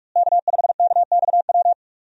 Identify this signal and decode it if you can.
Morse code, heard as a keyed tone, about 46 wpm, 700 Hz. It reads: K5KXW
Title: USA Call Signs - 5 Characters - Repeated 3x - 45wpm